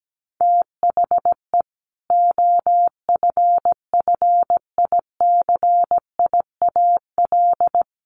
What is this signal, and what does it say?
Morse code, heard as a keyed tone, about 17 wpm, 700 Hz. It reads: THE OFFICIAL
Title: A Sound Of Thunder 17wpm